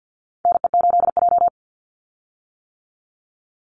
Morse code, heard as a keyed tone, about 50 wpm, 700 Hz. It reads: NE8J
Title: Call Signs - 50wpm